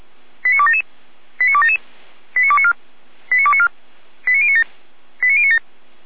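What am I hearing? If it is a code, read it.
Five-tone selective call (ZVEI), heard as quick runs of five tones, tones of about 70 ms. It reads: 89277 89277 89294 89294 89097 89097
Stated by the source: ZVEI five-tone selective call 89310, received on SDR.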